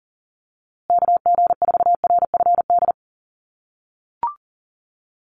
Morse code, heard as a keyed tone, about 40 wpm, 700 Hz. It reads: KG4RFD
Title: USA Call Signs Repeated 3x 40wpm